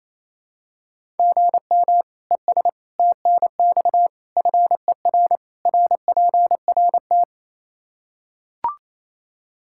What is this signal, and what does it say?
Morse code, heard as a keyed tone, about 27 wpm, 700 Hz. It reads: GM ES TNX FER RPRT
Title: K1USN SST QSO Practice 28wpm